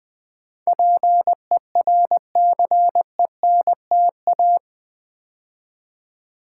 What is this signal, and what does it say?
Morse code, heard as a keyed tone, about 20 wpm, 690 Hz. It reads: PERCENTA
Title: Build A Word - 4 Letters At A Time - 10-Letter Words - 20wpm